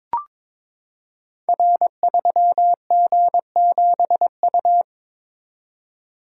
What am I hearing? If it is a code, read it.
Morse code, heard as a keyed tone, about 22 wpm, 700 Hz. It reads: R3G7U
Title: Code Groups - 5 Letter:Numbers - 22wpm